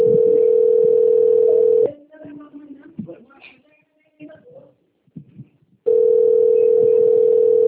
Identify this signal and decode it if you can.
Ringback tone (North America), 440 + 480 Hz, with speech.
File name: out- - - - - .85517